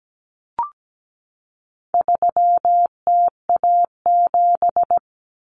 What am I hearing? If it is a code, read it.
Morse code, heard as a keyed tone, about 17 wpm, 700 Hz. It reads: 3TA7
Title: Common Cut Numbers - 4-Digit 17wpm